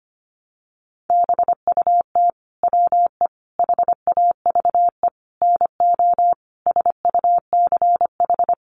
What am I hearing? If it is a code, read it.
Morse code, heard as a keyed tone, about 25 wpm, 700 Hz. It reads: BUT WE HAVE NO SUCH
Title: Sentences from Top 1300 Words 25wpm